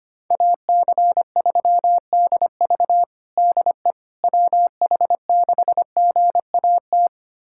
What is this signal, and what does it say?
Morse code, heard as a keyed tone, about 25 wpm, 700 Hz. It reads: AC3DV DE WH6GAT
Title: USA Callsign DE Callsign 25wpm